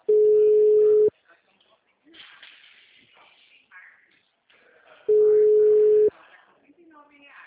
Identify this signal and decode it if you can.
Ringback tone (Europe, 425 Hz), with speech.
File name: out- -2019- - - .411